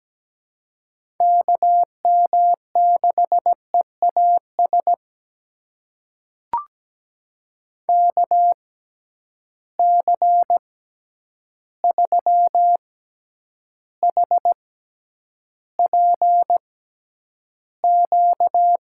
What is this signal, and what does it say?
Morse code, heard as a keyed tone, about 17 wpm, spaced as for 9 wpm, 700 Hz. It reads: KM6EAS KC3HPQ
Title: USA Call Signs - 6 Characters - 6x Character Spacing - 17wpm